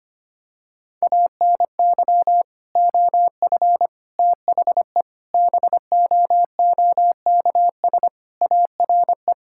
Morse code, heard as a keyed tone, about 25 wpm, 700 Hz. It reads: ANY OF THE BOOKS ARE
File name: practice / 25wpm / Sentences with 3-Word Phrases - 25wpm